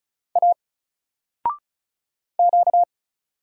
Morse code, heard as a keyed tone, about 34 wpm, 700 Hz. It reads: A Q
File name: Single-Letter-Number-35wpm